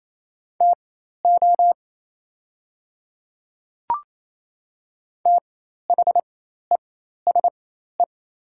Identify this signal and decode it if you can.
Morse code, heard as a keyed tone, about 28 wpm, spaced as for 13 wpm, 690 Hz. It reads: TO THESE
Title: 4x Character Spacing - Top 100 Words - 28wpm